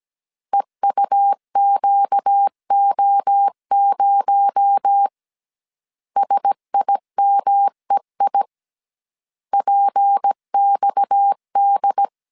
Morse code, heard as a keyed tone, about 17 wpm, 800 Hz. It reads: EUQO0 SIMEI PXD
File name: CW / DET / 1 6 1500C S A 16wpm 2635